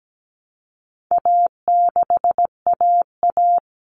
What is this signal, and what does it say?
Morse code, heard as a keyed tone, about 17 wpm, 700 Hz. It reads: A6AA
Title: All Cut Numbers - 4-Digit 17wpm